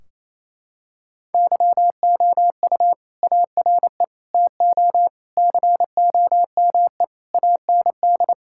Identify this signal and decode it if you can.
Morse code, heard as a keyed tone, about 28 wpm, 700 Hz. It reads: YOU ARE TO COME AND
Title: Sentences from Top 1600 Words 28wpm